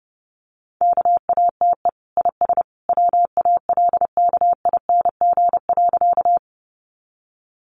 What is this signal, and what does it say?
Morse code, heard as a keyed tone, about 30 wpm, 700 Hz. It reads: KATE IS WALKING.
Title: Easy Phrases 30wpm